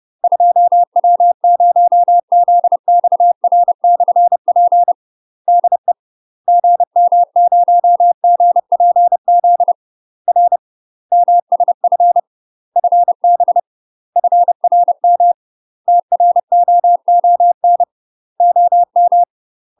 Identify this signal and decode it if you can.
Morse code, heard as a keyed tone, about 30 wpm, 700 Hz. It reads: 2W0ZXR/P DE GM0GPZ R MSF FB FRM TROON OM
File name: UK MASTER QSOs TRAINING/30WPM